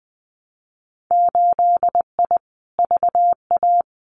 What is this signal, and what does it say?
Morse code, heard as a keyed tone, about 20 wpm, 700 Hz. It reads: 8I VA
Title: Winter Field Day Exchanges Encore 20wpm